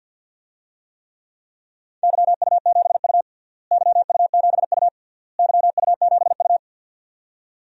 Morse code, heard as a keyed tone, about 47 wpm, 700 Hz. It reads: YU7U YU7U YU7U
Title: DXCC-100 Call Signs - 4 Characters - Repeated 3x - 50wpm